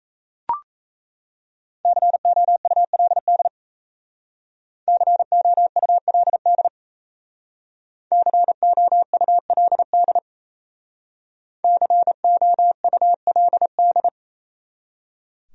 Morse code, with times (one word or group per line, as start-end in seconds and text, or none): 1.9-3.5 COULD
4.9-6.7 COULD
8.1-10.2 COULD
11.6-14.1 COULD